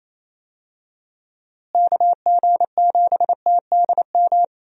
Morse code, heard as a keyed tone, about 28 wpm, 700 Hz. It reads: KG7TDM
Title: USA Call Signs - 6 Characters - Repeated 2x - 28wpm